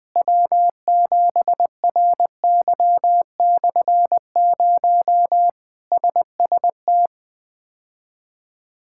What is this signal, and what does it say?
Morse code, heard as a keyed tone, about 20 wpm, 700 Hz. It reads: W7RY/0 SST
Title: K1USN SST QSO Practice - 20wpm